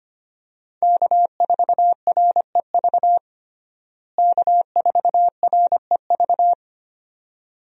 Morse code, heard as a keyed tone, about 25 wpm, 700 Hz. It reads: K4REV K4REV
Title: USA Call Signs - 5 Characters - Repeated 2x - 25wpm